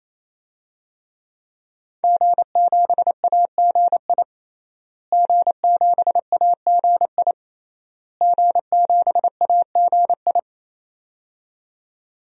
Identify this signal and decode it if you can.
Morse code, heard as a keyed tone, about 28 wpm, 700 Hz. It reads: G7AGI G7AGI G7AGI
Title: DXCC-100 Call Signs - 5 Characters - Repeated 3x - 28wpm